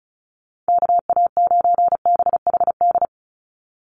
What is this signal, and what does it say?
Morse code, heard as a keyed tone, about 34 wpm, 700 Hz. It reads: KA9BHD
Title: Call Signs - 35wpm